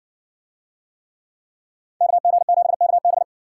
Morse code, heard as a keyed tone, about 58 wpm, 700 Hz. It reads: KG7KB